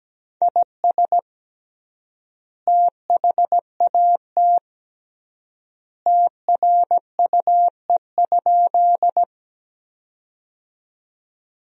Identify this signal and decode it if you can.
Morse code, heard as a keyed tone, about 17 wpm, 700 Hz. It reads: IS THAT TRUE?
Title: 3x Word-Spacing - Sentences from Top 400 Words - 17wpm